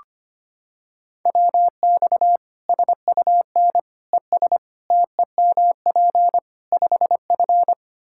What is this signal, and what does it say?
Morse code, heard as a keyed tone, about 25 wpm, 700 Hz. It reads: WX SUN ES TEMP 5F
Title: Practice QSO - Exchange 2 - Info - 25wpm